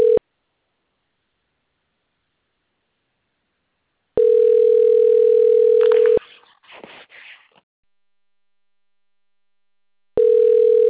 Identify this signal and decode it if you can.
Ringback tone (North America), 440 + 480 Hz.